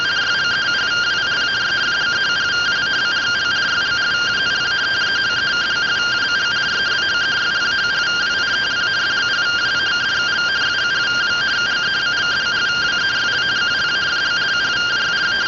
Two-tone teleprinter signal, about 170 Hz shift.
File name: findings / S ss-portrait.gen-14 rtty rx